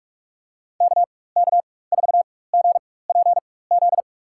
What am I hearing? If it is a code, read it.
Morse code, heard as a keyed tone, about 46 wpm, spaced as for 21 wpm, 700 Hz. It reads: KK4GPZ
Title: USA Call Signs - 6 Characters - 4x Character Spacing - 45wpm